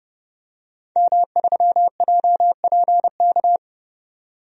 Morse code, heard as a keyed tone, about 29 wpm, 700 Hz. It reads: M3JPK
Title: DXCC-100 Call Signs - 5 Characters - 30wpm